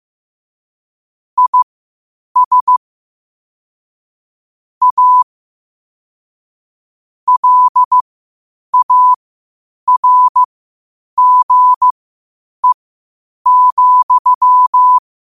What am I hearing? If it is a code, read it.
Morse code, heard as a keyed tone, about 15 wpm, spaced as for 8 wpm, 1000 Hz. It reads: IS A LARGE,
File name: qotdfiles / 8wpm